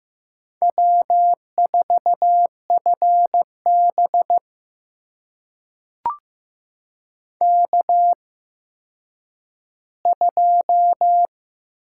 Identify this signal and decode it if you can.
Morse code, heard as a keyed tone, about 15 wpm, spaced as for 9 wpm, 700 Hz. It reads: W4FB K2
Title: USA Call Signs - 4 Characters - 8x Character Spacing - 15wpm